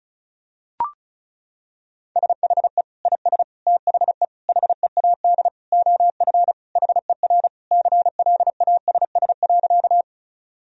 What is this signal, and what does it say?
Morse code, heard as a keyed tone, about 36 wpm, 700 Hz. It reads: SHE IS THE HEAD OF HER CLASS.
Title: Sentences from Top 1000 Words - Review Part 2 - 35wpm